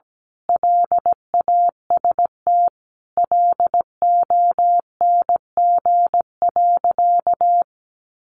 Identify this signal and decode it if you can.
Morse code, heard as a keyed tone, about 17 wpm, 700 Hz. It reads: LAST LONG.
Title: Sentences from Top 1300 Words 17wpm